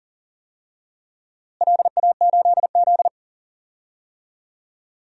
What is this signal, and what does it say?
Morse code, heard as a keyed tone, about 40 wpm, 700 Hz. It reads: LA8Z
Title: DXCC-100 Call Signs - 4 Characters - Repeated 2x - 40wpm